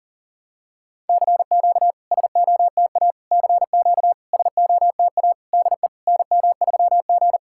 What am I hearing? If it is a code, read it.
Morse code, heard as a keyed tone, about 43 wpm, 690 Hz. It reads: CQ SOTA CQ SOTA DE NM3G
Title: SOTA QSO Practice 40wpm